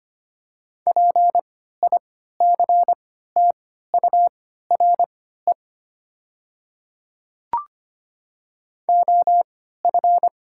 Morse code, heard as a keyed tone, about 25 wpm, spaced as for 14 wpm, 700 Hz. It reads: PICTURE OF
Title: 3x Character Spacing - Top 200 Words - 25wpm